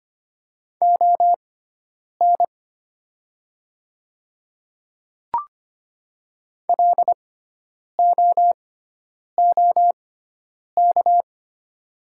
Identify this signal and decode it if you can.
Morse code, heard as a keyed tone, about 25 wpm, spaced as for 9 wpm, 700 Hz. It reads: ON LOOK
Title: 6x Character Spacing - Top 100 Words - 25wpm